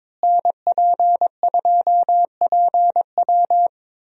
Morse code, heard as a keyed tone, about 21 wpm, 700 Hz. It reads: NP2PW
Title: USA Call Signs - 5 Characters - Repeated 3x - 22wpm